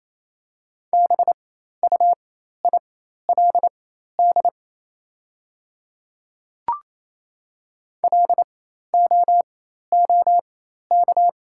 Morse code, heard as a keyed tone, about 28 wpm, spaced as for 13 wpm, 700 Hz. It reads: BUILD LOOK
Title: 4x Character Spacing - Top 200 Words - 28wpm